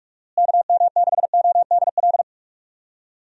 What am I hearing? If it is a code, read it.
Morse code, heard as a keyed tone, about 42 wpm, 700 Hz. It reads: KM6ODL
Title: USA Call Signs - 6 Characters - Repeated 2x - 45wpm